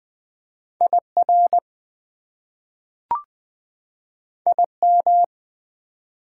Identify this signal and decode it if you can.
Morse code, heard as a keyed tone, about 20 wpm, 700 Hz. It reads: IR IM
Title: Prefixes and Suffixes 20wpm